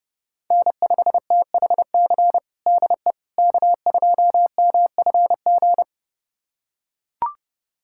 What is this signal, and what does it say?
Morse code, heard as a keyed tone, about 30 wpm, 700 Hz. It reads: N5THC DE K2MFG
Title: USA Callsign DE Callsign - 5 Characters - 30wpm